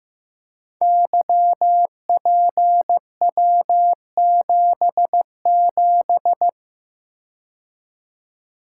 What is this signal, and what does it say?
Morse code, heard as a keyed tone, about 15 wpm, 700 Hz. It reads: YPW77
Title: Code Groups - 5 Letter:Numbers - 15wpm